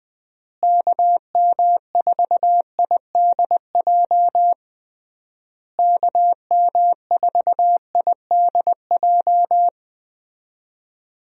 Morse code, heard as a keyed tone, about 20 wpm, 700 Hz. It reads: KM4IDJ KM4IDJ
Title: USA Call Signs - 6 Characters - Repeated 3x - 20wpm